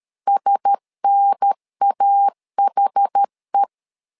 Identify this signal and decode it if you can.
Morse code, heard as a keyed tone, about 13 wpm, 800 Hz. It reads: SNAHE
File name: LEZIONI / 5 LBVC / 5 LBV 12wpm 1855